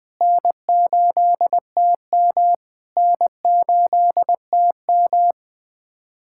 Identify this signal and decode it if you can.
Morse code, heard as a keyed tone, about 20 wpm, 700 Hz. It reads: N8TM N8TM
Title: SOTA QSO Practice 20wpm